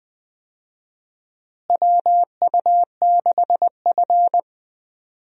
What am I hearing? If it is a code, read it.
Morse code, heard as a keyed tone, about 21 wpm, 700 Hz. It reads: WU6F
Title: USA Call Signs - 4 Characters - Repeated 3x - 20wpm